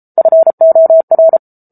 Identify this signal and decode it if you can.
Morse code, heard as a keyed tone, about 34 wpm, 650 Hz. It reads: FOR
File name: Book Two / 07-Chapter 7 - 33 WPM